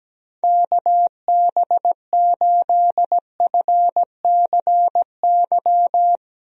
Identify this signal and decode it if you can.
Morse code, heard as a keyed tone, about 17 wpm, 700 Hz. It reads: KB8FCY
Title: USA Call Signs - 6 Characters - 17wpm